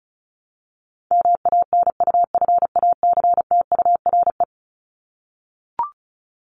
Morse code, heard as a keyed tone, about 34 wpm, 700 Hz. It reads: MANUFACTURE
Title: ING Suffix - Top 100 Words with Base Words 35wpm